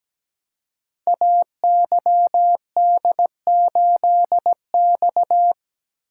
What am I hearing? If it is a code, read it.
Morse code, heard as a keyed tone, about 17 wpm, 700 Hz. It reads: AYD8X
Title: Code Groups - 5 Letter:Numbers - 17wpm